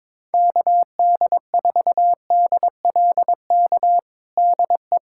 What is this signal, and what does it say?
Morse code, heard as a keyed tone, about 21 wpm, 700 Hz. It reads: KD4DLK DE
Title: USA Callsign DE Callsign - 6 Characters - 22wpm